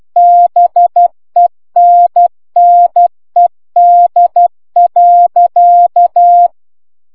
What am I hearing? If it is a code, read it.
Morse code, heard as a keyed tone, about 12 wpm, 700 Hz. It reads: BENNED.